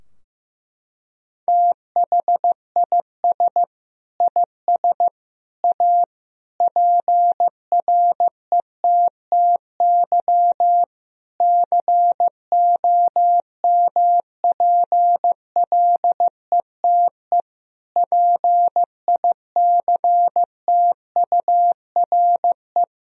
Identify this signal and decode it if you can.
Morse code, heard as a keyed tone, about 15 wpm, 700 Hz. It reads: THIS IS A PRETTY COMPLETE PICTURE
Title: Sentences from Top 1000 Words - Review Part 2 - 15wpm